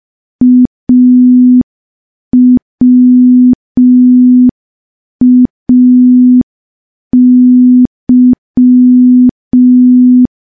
Morse code, heard as a keyed tone, about 5 wpm, 250 Hz. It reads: AWAY
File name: morse / morse-training-05WPM